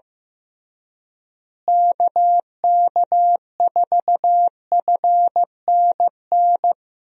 Morse code, heard as a keyed tone, about 15 wpm, 700 Hz. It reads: KK4FNN
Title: USA Call Signs - 6 Characters - Repeated 2x - 15wpm